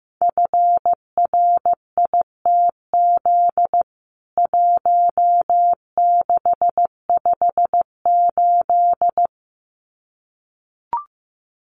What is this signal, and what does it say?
Morse code, heard as a keyed tone, about 15 wpm, 700 Hz. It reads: FRITZ 1658
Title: CWT Exchange 15wpm